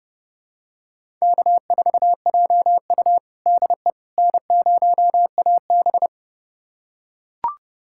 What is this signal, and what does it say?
Morse code, heard as a keyed tone, about 32 wpm, 700 Hz. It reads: K4JU DE N0AB
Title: USA Callsign DE Callsign - 4 Characters - 30wpm